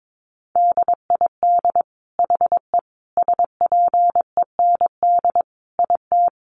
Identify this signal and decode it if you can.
Morse code, heard as a keyed tone, about 22 wpm, 700 Hz. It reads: DID HE SPEND IT
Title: Sentences from Top 800 Words 22wpm